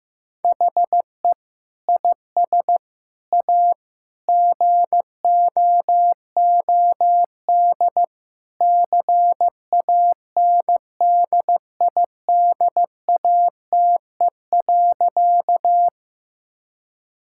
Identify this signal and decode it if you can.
Morse code, heard as a keyed tone, about 15 wpm, 700 Hz. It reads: HE IS A GOOD CANDIDATE.
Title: Sentences from Top 1600 Words 15wpm